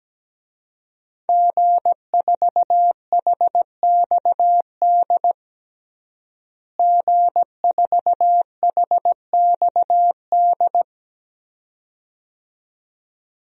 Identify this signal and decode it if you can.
Morse code, heard as a keyed tone, about 17 wpm, 700 Hz. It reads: G4HXD G4HXD
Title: DXCC-100 Call Signs - 5 Characters - Repeated 2x - 17wpm